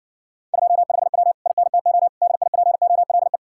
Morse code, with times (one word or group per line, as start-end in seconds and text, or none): 0.5-1.3 25W
1.5-2.1 INTO
2.2-3.4 DIPOLE